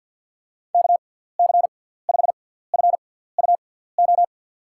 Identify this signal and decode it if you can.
Morse code, heard as a keyed tone, about 47 wpm, spaced as for 17 wpm, 700 Hz. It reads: KC5FUO
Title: USA Call Signs - 6 Characters - 6x Character Spacing - 50wpm